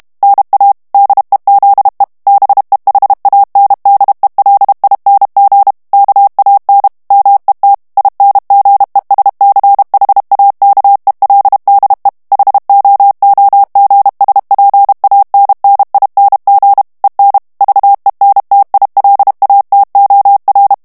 Morse code, heard as a keyed tone, about 32 wpm, 800 Hz. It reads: NA DEZE BEHANDELING KAN MET INGESCHAKELDE HOOGSPANNING EN VENTILATOR
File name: HOLLANDAIS texte clair 2 vitesse 30 mots minute